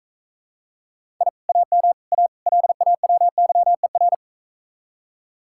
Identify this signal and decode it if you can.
Morse code, heard as a keyed tone, about 43 wpm, 700 Hz. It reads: I AM A LAWYER